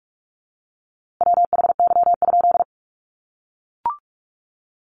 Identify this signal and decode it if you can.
Morse code, heard as a keyed tone, about 44 wpm, 700 Hz. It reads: WHY?
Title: A Sound Of Thunder 45wpm